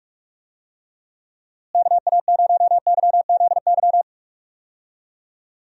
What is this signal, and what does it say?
Morse code, heard as a keyed tone, about 41 wpm, 700 Hz. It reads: KA0YZY
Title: USA Call Signs - 6 Characters - Repeated 3x - 45wpm